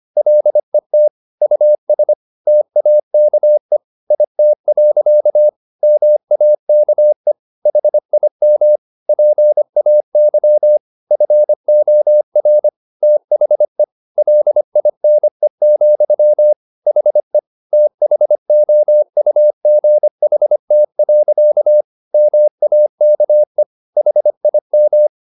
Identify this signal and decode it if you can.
Morse code, heard as a keyed tone, about 25 wpm, 600 Hz. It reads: LET US TAKE IT. MAKE HIM PAY FOR THE LINE, HE THOUGHT. MAKE HIM